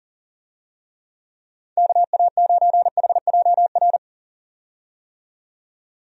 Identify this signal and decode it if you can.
Morse code, heard as a keyed tone, about 40 wpm, 700 Hz. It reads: KA9HJR
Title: USA Call Signs - 6 Characters - 40wpm